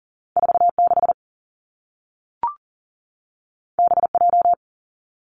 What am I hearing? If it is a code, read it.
Morse code, heard as a keyed tone, about 42 wpm, 700 Hz. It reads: 4 6 B J
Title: Two Challenging Letter-Number Combinations 40wpm